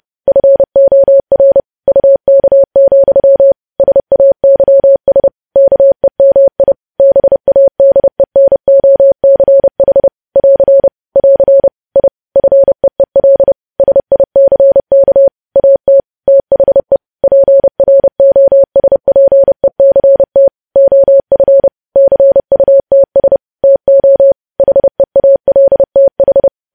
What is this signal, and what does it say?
Morse code, heard as a keyed tone, about 30 wpm, 550 Hz. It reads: FOR UK, SAYS KEMI BADENOCH <AR> <AR> I FEEL SICK AT THE PROSPECT OF CUTS TO HEALTH
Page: news headlines in morse code at 30 wpm